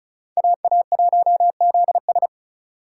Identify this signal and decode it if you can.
Morse code, heard as a keyed tone, about 44 wpm, 700 Hz. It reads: AA1ZS